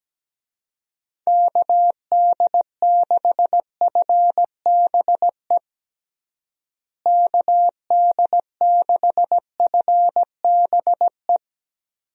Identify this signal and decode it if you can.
Morse code, heard as a keyed tone, about 17 wpm, 700 Hz. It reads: KD6FBE KD6FBE
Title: USA Call Signs - 6 Characters - Repeated 2x - 17wpm